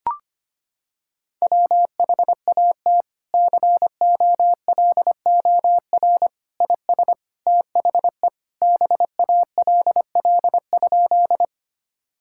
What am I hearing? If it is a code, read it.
Morse code, heard as a keyed tone, about 25 wpm, 700 Hz. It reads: WHAT COLOR IS THE BALL?
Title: Sentences from Top 1000 Words - Review Part 6 - 25wpm